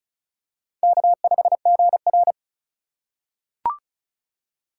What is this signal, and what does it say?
Morse code, heard as a keyed tone, about 36 wpm, 700 Hz. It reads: K5GR
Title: USA Call Signs - 4 Characters - 4x Character Spacing - 35wpm